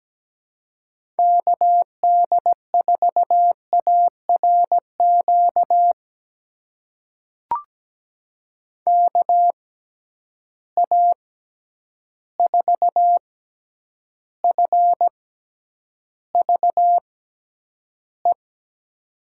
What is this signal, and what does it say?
Morse code, heard as a keyed tone, about 17 wpm, spaced as for 9 wpm, 700 Hz. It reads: KD4ARQ KA4FVE
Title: USA Call Signs - 6 Characters - 6x Character Spacing - 17wpm